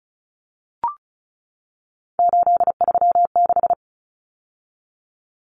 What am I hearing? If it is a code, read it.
Morse code, heard as a keyed tone, about 34 wpm, 700 Hz. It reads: 836
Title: Numbers - 3-Digit 35wpm